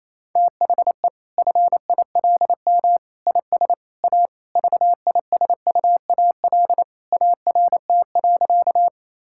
Morse code, heard as a keyed tone, about 28 wpm, 700 Hz. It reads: THE FILM IS A VISUAL ART.
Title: Sentences from Top 1000 Words - Review Part 6 - 28wpm